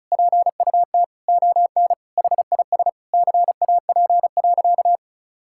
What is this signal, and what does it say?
Morse code, heard as a keyed tone, about 36 wpm, 700 Hz. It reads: PUT ON HIS CAP.
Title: Sentences from Top 1600 Words 35wpm